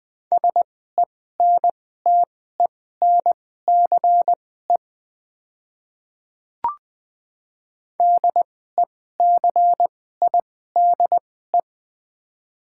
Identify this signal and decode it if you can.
Morse code, heard as a keyed tone, about 20 wpm, spaced as for 15 wpm, 700 Hz. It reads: SENTENCE DECIDE
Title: 2x Character Spacing - Top 500 Words - 20wpm